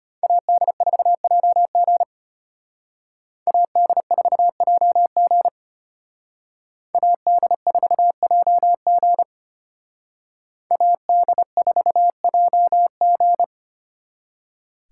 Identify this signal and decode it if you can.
Morse code, heard as a keyed tone, about 32 wpm, 700 Hz. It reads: AD4JG AD4JG AD4JG AD4JG